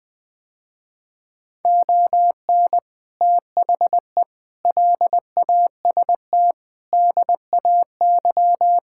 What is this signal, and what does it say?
Morse code, heard as a keyed tone, about 20 wpm, 700 Hz. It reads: ON THE LAST DAY
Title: Sentences with 6-Word Phrases - Spoken Beginning - 20wpm